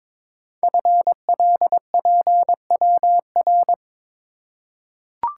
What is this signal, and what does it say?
Morse code, heard as a keyed tone, about 22 wpm, 700 Hz. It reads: FLPWR
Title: Code Groups - 5 Letters - 22wpm